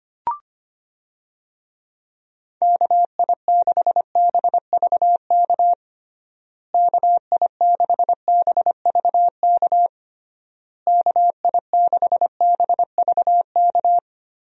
Morse code, heard as a keyed tone, about 25 wpm, 700 Hz. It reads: KI6BVK KI6BVK KI6BVK
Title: USA Call Signs - 6 Characters - Repeated 3x - 25wpm